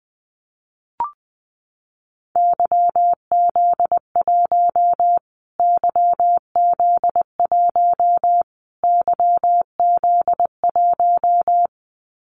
Morse code, heard as a keyed tone, about 19 wpm, 700 Hz. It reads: YZ1 YZ1 YZ1
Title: Callsign Prefixes for Top 25 DXCC - Repeated 3x - 20wpm